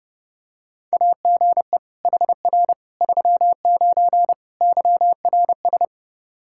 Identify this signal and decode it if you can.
Morse code, heard as a keyed tone, about 32 wpm, 700 Hz. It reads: AGE HR 39 YRS
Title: Practice QSO - Exchange 3 - More Info - 30wpm